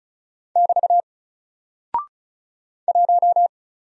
Morse code, heard as a keyed tone, about 36 wpm, 700 Hz. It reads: <BT> 1
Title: Single Character (Letter:Number:Punctuation:Prosign) 35wpm